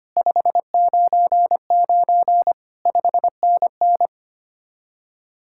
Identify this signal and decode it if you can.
Morse code, heard as a keyed tone, about 32 wpm, 700 Hz. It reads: 599 5NN